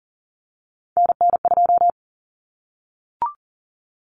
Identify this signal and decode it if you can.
Morse code, heard as a keyed tone, about 40 wpm, 700 Hz. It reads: NN2
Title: Common Cut Numbers - 3-Digit 40wpm